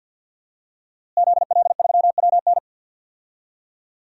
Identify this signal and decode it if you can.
Morse code, heard as a keyed tone, about 50 wpm, 700 Hz. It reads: ZL3WN
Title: DXCC-100 Call Signs - 5 Characters - Repeated 3x - 50wpm